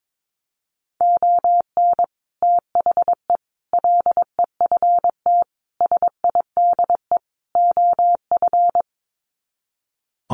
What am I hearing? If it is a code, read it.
Morse code, heard as a keyed tone, about 22 wpm, 700 Hz. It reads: ON THE LEFT SIDE OF
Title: Sets of 5 Words Encore2 22wpm